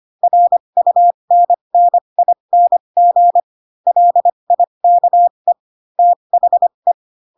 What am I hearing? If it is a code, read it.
Morse code, heard as a keyed tone, about 25 wpm, spaced as for 22 wpm, 700 Hz. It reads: RUNNING LIKE THE
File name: Pinocchio / Pinocchio ch3 18 25 ik1hge